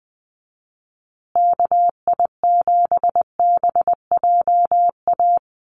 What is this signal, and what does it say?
Morse code, heard as a keyed tone, about 20 wpm, 700 Hz. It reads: KI7BJA
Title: USA Call Signs Repeated 3x 20wpm